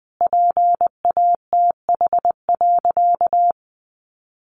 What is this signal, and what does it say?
Morse code, heard as a keyed tone, about 20 wpm, 700 Hz. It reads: PATH.
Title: Sentences from Top 1400 Words 20wpm